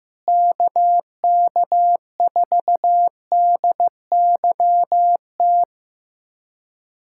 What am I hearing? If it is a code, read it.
Morse code, heard as a keyed tone, about 15 wpm, 700 Hz. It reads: KK4DYT
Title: USA Callsign DE Callsign 15wpm